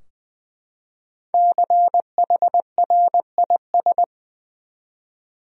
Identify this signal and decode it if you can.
Morse code, heard as a keyed tone, about 20 wpm, 700 Hz. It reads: CHRIS
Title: USA Common Names - Max 5 Letters - 20wpm